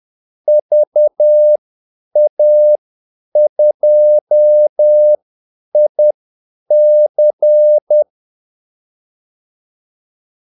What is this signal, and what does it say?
Morse code, heard as a keyed tone, about 10 wpm, spaced as for 8 wpm, 600 Hz. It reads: VA2IC
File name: Callsigns-10-8wpm